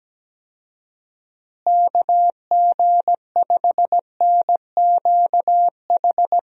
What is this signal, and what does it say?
Morse code, heard as a keyed tone, about 17 wpm, 700 Hz. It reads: KG5NQH
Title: USA Call Signs - 6 Characters - Repeated 2x - 17wpm